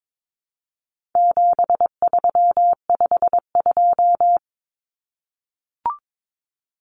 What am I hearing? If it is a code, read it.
Morse code, heard as a keyed tone, about 22 wpm, 700 Hz. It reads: 7352
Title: Numbers - 4-Digit 22wpm